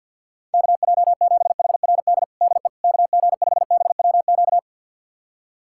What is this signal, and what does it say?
Morse code, heard as a keyed tone, about 48 wpm, 700 Hz. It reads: KJ7HRD DE KG5BWQ
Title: USA Callsign DE Callsign 50wpm